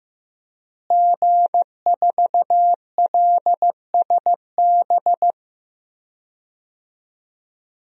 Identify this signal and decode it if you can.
Morse code, heard as a keyed tone, about 15 wpm, 700 Hz. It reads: G4LSB
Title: DXCC-100 Call Signs - 5 Characters - Repeated 3x - 15wpm